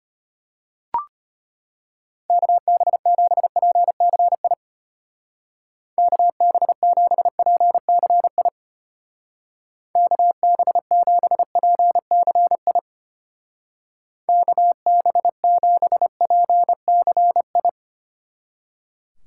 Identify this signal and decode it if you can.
Morse code, heard as a keyed tone, about 32 wpm, 700 Hz. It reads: KB7PCI KB7PCI KB7PCI KB7PCI